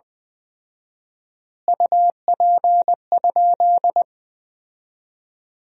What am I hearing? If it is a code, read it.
Morse code, heard as a keyed tone, about 20 wpm, 700 Hz. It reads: UP?
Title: 4x Word-Spacing - Sentences from Top 200 Words - 20wpm